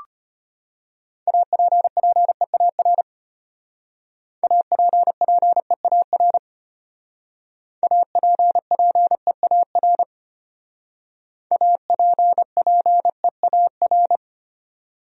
Morse code, heard as a keyed tone, about 32 wpm, 700 Hz. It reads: APPEAR APPEAR APPEAR APPEAR